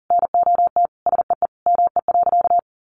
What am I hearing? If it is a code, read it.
Morse code, heard as a keyed tone, about 40 wpm, 700 Hz. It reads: NOT SEE ME.
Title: Sentences from Top 600 Words 40wpm